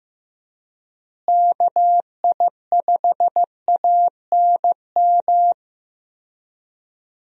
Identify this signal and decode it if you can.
Morse code, heard as a keyed tone, about 15 wpm, 700 Hz. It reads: KI5ANM
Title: USA Call Signs - 6 Characters - Repeated 3x - 15wpm